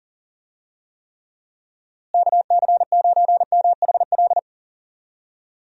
Morse code, heard as a keyed tone, about 40 wpm, 700 Hz. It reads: KC9MHL
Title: USA Call Signs - 6 Characters - Repeated 2x - 40wpm